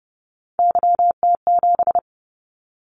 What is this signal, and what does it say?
Morse code, heard as a keyed tone, about 30 wpm, 700 Hz. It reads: YT7
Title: Callsign Prefixes for Top 25 DXCC - 30wpm